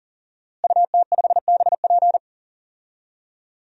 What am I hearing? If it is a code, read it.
Morse code, heard as a keyed tone, about 40 wpm, 700 Hz. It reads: UT5BP
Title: DXCC-100 Call Signs - 5 Characters - Repeated 3x - 40wpm